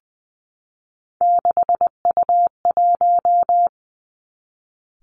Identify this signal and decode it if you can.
Morse code, heard as a keyed tone, about 20 wpm, 700 Hz. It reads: 6U1
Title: Three Challenging Letter-Number Combinations 20wpm